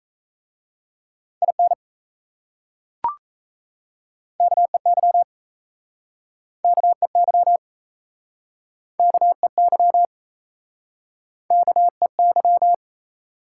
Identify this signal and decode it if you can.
Morse code, heard as a keyed tone, about 38 wpm, 700 Hz. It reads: IN KEY KEY KEY KEY